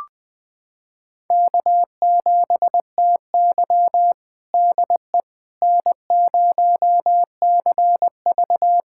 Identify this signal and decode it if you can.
Morse code, heard as a keyed tone, about 20 wpm, 700 Hz. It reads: K7TY DE N0CV
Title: USA Callsign DE Callsign - 4 Characters - 20wpm